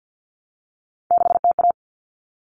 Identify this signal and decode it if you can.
Morse code, heard as a keyed tone, about 50 wpm, 700 Hz. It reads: 6TA
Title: Common Cut Numbers - 3-Digit 50wpm